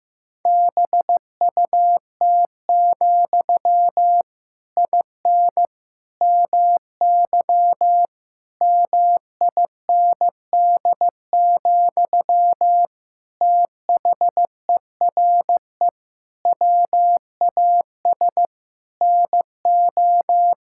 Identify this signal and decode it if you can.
Morse code, heard as a keyed tone, about 15 wpm, 700 Hz. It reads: BUT, IN MY MIND, THERE WAS NO
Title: Sentences from Top 1000 Words - Review Part 5 - 15wpm